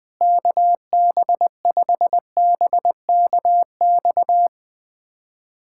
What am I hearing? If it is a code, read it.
Morse code, heard as a keyed tone, about 20 wpm, 700 Hz. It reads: KB5BKX
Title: USA Callsign DE Callsign 20wpm